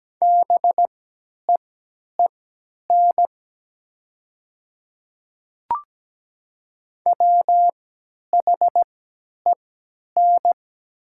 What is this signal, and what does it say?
Morse code, heard as a keyed tone, about 17 wpm, spaced as for 10 wpm, 700 Hz. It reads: BEEN WHEN
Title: 3x Character Spacing - Top 300 Words - 17wpm